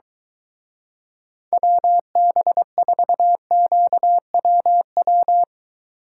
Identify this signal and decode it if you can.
Morse code, heard as a keyed tone, about 23 wpm, 700 Hz. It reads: WB4QWW